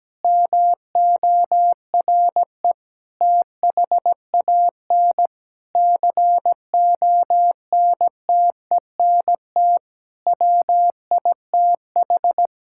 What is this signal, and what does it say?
Morse code, heard as a keyed tone, about 17 wpm, 700 Hz. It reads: MORE THAN CONTENT WITH
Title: Sentences from Top 1000 Words - Review Part 6 - 17wpm